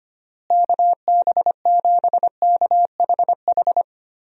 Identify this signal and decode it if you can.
Morse code, heard as a keyed tone, about 25 wpm, 700 Hz. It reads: KB7KHH
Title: USA Call Signs - 6 Characters - Repeated 2x - 25wpm